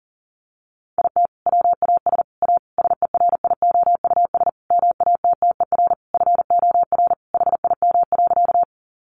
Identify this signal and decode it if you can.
Morse code, heard as a keyed tone, about 40 wpm, 700 Hz. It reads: IT WAS A SERIOUS MATTER FOR HIM.
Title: Sentences from Top 1100 Words 40wpm